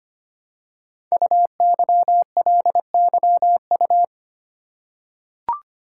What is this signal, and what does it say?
Morse code, heard as a keyed tone, about 25 wpm, 700 Hz. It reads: UYLYU
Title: Code Groups - 5 Letters - 25wpm